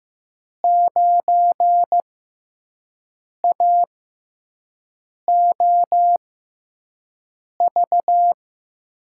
Morse code, heard as a keyed tone, about 15 wpm, spaced as for 5 wpm, 700 Hz. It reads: 9AOV
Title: USA Call Signs - 5 Characters - 6x Character Spacing - 15wpm